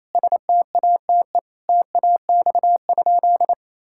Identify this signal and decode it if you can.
Morse code, heard as a keyed tone, about 28 wpm, 700 Hz. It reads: STATE TAX?
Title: Sentences from Top 1000 Words - Review Part 5 - 28wpm